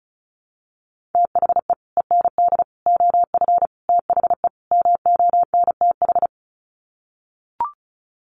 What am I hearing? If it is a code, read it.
Morse code, heard as a keyed tone, about 34 wpm, 700 Hz. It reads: THE END OF THE MONTH
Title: Sets of 5 Words Encore 35wpm